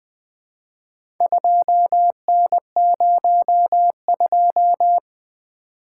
Morse code, heard as a keyed tone, about 20 wpm, 700 Hz. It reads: 2N02
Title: Four Letter-Number Combinations - 20wpm